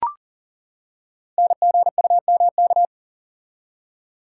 Morse code, heard as a keyed tone, about 38 wpm, 700 Hz. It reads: NGUMK
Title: Code Groups - 5 Letters - 40wpm